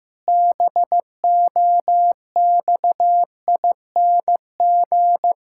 Morse code, heard as a keyed tone, about 15 wpm, 700 Hz. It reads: BOXING
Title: Pangram Challenge v2 - 15wpm